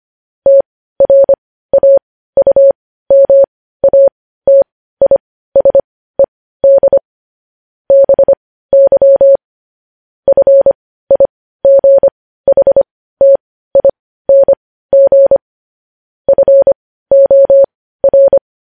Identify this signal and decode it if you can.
Morse code, heard as a keyed tone, about 25 wpm, spaced as for 15 wpm, 550 Hz. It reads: TRAUMATISED BY FIGHTING FOR
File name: podcast / News headlines-1x-25x15